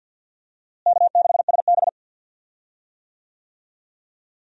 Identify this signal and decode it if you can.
Morse code, heard as a keyed tone, about 46 wpm, 700 Hz. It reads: K6SB
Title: USA Call Signs - 4 Characters - Repeated 3x - 50wpm